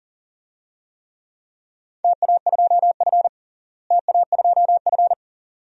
Morse code, heard as a keyed tone, about 37 wpm, 700 Hz. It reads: TA2F TA2F
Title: DXCC-100 Call Signs - 4 Characters - Repeated 2x - 40wpm